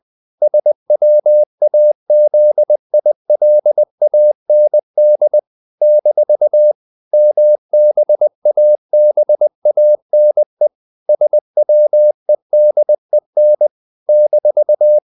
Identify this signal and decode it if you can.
Morse code, heard as a keyed tone, about 20 wpm, 600 Hz. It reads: SWAZILAND - MBABANE SWEDEN -
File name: CW / Practice / NationalCapitals20